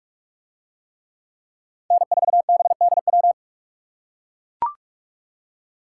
Morse code, heard as a keyed tone, about 41 wpm, 700 Hz. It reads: N4BDW